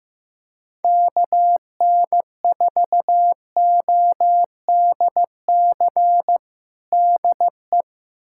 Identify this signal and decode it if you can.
Morse code, heard as a keyed tone, about 16 wpm, 700 Hz. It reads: KN4ODC DE
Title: USA Callsign DE Callsign 15wpm